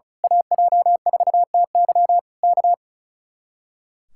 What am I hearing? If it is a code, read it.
Morse code, heard as a keyed tone, about 33 wpm, 700 Hz. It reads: AJ4TY K
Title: Speed-Racing Common QSO Phrases - 35wpm